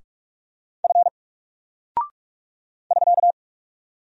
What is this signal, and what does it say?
Morse code, heard as a keyed tone, about 46 wpm, 700 Hz. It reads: F <SK>
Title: Single Character (Letter:Number:Punctuation:Prosign) 45wpm